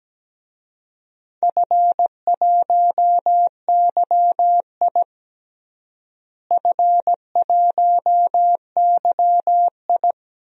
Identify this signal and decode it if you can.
Morse code, heard as a keyed tone, about 17 wpm, 700 Hz. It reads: F1YI F1YI
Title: DXCC-100 Call Signs - 4 Characters - Repeated 3x - 17wpm